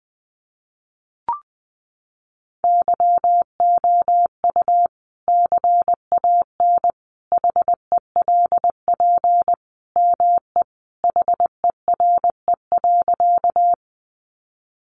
Morse code, heard as a keyed tone, about 20 wpm, 700 Hz. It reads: YOU CAN HELP ME HERE.
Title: Sentences from Top 200 Words 20wpm